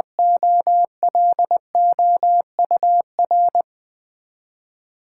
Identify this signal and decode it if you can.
Morse code, heard as a keyed tone, about 20 wpm, 700 Hz. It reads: OLOUR
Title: Build A Word - 3 Letters At A Time - 6-Letter Words - 20wpm